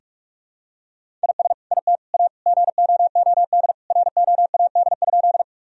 Morse code, heard as a keyed tone, about 46 wpm, 700 Hz. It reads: IS IT A GOOD ROAD?
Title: Sentences from Top 1000 Words - Review Part 2 - 45wpm